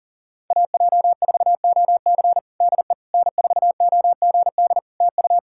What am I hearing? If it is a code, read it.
Morse code, heard as a keyed tone, about 37 wpm, 700 Hz. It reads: AJ4OC DE N4OGD TU
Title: Practice QSO - Exchange 4 - Ending - 40wpm